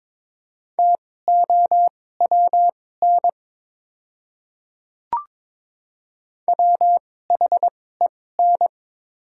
Morse code, heard as a keyed tone, about 22 wpm, spaced as for 16 wpm, 700 Hz. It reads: TOWN WHEN
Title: 2x Character Spacing - Top 500 Words - 22wpm